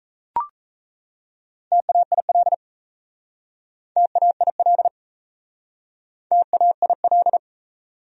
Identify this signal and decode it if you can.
Morse code, heard as a keyed tone, about 37 wpm, 700 Hz. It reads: TAIL TAIL TAIL